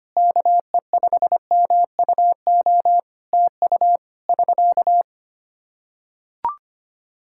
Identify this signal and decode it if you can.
Morse code, heard as a keyed tone, about 25 wpm, 700 Hz. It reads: KE5MUO TU <SK>
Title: K1USN SST QSO Practice 25wpm